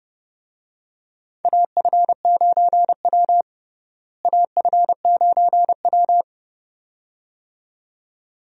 Morse code, heard as a keyed tone, about 30 wpm, 700 Hz. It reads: AF9W AF9W
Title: USA Call Signs - 4 Characters - Repeated 2x - 30wpm